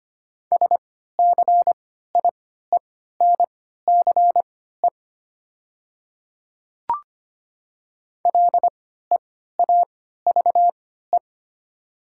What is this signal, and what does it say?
Morse code, heard as a keyed tone, about 25 wpm, spaced as for 14 wpm, 690 Hz. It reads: SCIENCE LEAVE
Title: 3x Character Spacing - Top 500 Words - 25wpm